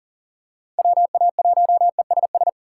Morse code, heard as a keyed tone, about 40 wpm, 700 Hz. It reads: WA1ESS
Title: USA Call Signs - 6 Characters - 3x Character Spacing - 40wpm